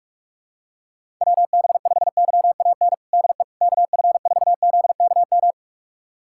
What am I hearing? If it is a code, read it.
Morse code, heard as a keyed tone, about 47 wpm, 700 Hz. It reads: WB5YAN DE KF4ZKM
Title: USA Callsign DE Callsign 45wpm